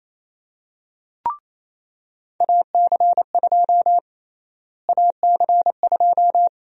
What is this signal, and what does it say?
Morse code, heard as a keyed tone, about 27 wpm, 700 Hz. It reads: AC2 AC2
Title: USA Call Sign Prefixes - Repeated 2x - 28wpm